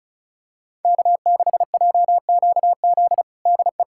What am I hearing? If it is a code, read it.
Morse code, heard as a keyed tone, about 36 wpm, 690 Hz. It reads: K6JQZ DE
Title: USA Callsign DE Callsign - 5 Characters - 35wpm